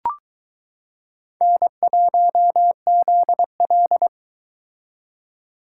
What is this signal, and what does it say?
Morse code, heard as a keyed tone, about 22 wpm, 700 Hz. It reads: N1ZL